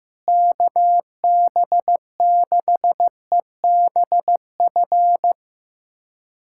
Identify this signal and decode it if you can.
Morse code, heard as a keyed tone, about 16 wpm, 700 Hz. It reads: KB6EBF
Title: USA Callsign DE Callsign 15wpm